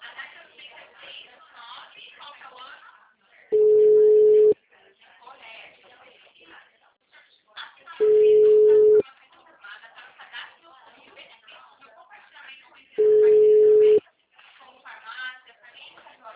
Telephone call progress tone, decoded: ringback tone

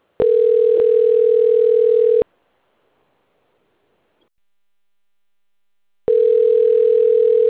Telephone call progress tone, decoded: ringback tone

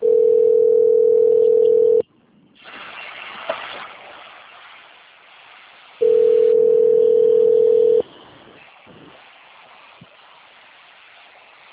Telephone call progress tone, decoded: ringback tone

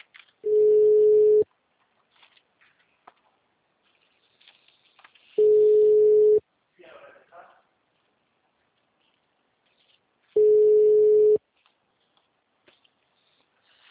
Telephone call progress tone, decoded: ringback tone

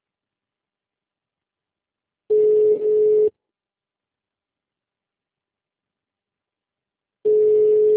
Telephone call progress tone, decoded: ringback tone

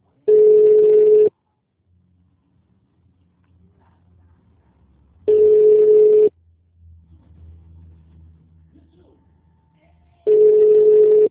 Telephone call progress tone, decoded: ringback tone

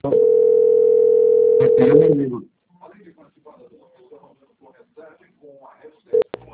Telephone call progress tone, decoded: ringback tone

